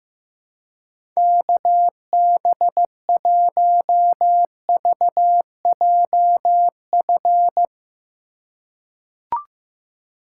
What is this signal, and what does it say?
Morse code, heard as a keyed tone, about 15 wpm, 700 Hz. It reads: KB1VJF
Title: USA Call Signs - 6 Characters - 3x Character Spacing - 15wpm